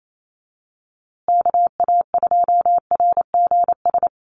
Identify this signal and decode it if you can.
Morse code, heard as a keyed tone, about 28 wpm, 700 Hz. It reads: KA2RGS
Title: USA Call Signs Repeated 3x 28wpm